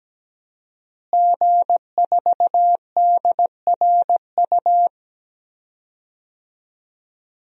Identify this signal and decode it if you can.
Morse code, heard as a keyed tone, about 17 wpm, 700 Hz. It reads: G4DRU
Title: DXCC-100 Call Signs - 5 Characters - Repeated 3x - 17wpm